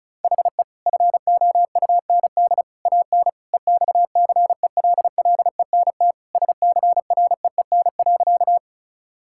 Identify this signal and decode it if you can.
Morse code, heard as a keyed tone, about 34 wpm, 700 Hz. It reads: HE FOUND AN EXCELLENT SCREEN.
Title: Sentences from Top 1000 Words - Review Part 6 - 35wpm